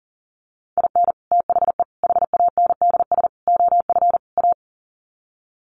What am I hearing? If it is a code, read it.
Morse code, heard as a keyed tone, about 40 wpm, 700 Hz. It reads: IN THE HANDS OF A
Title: Sets of 5 Words Encore 40wpm